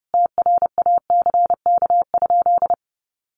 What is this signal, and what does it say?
Morse code, heard as a keyed tone, about 30 wpm, 700 Hz. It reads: TRACK?
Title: Sentences from Top 1100 Words 30wpm